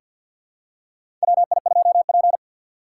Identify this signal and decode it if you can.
Morse code, heard as a keyed tone, about 47 wpm, 700 Hz. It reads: WI2P
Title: USA Call Signs - 4 Characters - 2x Character Spacing - 50wpm